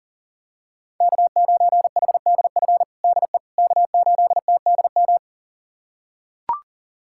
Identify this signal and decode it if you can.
Morse code, heard as a keyed tone, about 40 wpm, 700 Hz. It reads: K9HDF DE K8TDM
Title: USA Callsign DE Callsign - 5 Characters - 40wpm